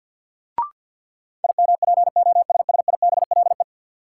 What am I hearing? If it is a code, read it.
Morse code, heard as a keyed tone, about 50 wpm, 700 Hz. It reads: IMPOSSIBLE
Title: Sentences from Top 1100 Words - New Word Drill Alt - 50wpm